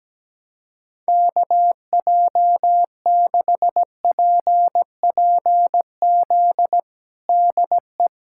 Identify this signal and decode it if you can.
Morse code, heard as a keyed tone, about 17 wpm, 700 Hz. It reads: KJ6PPZ DE
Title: USA Callsign DE Callsign 17wpm